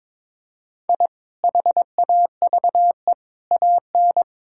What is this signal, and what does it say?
Morse code, heard as a keyed tone, about 22 wpm, 700 Hz. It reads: I HAVE AN
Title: Sentences from Top 1000 Words - Review Part 6 - 22wpm